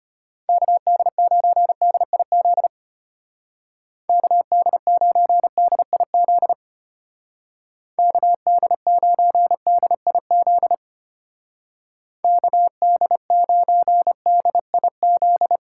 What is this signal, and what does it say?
Morse code, heard as a keyed tone, about 33 wpm, 700 Hz. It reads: KD9DIZ KD9DIZ KD9DIZ KD9DIZ